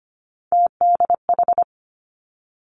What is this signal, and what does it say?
Morse code, heard as a keyed tone, about 25 wpm, 700 Hz. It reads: TDH
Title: Three Challenging Letter-Number Combinations 25wpm